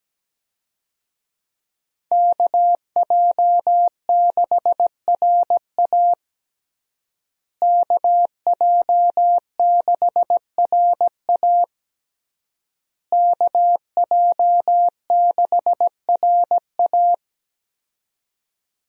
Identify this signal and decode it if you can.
Morse code, heard as a keyed tone, about 17 wpm, 700 Hz. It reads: KJ6RA KJ6RA KJ6RA
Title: USA Call Signs - 5 Characters - Repeated 3x - 17wpm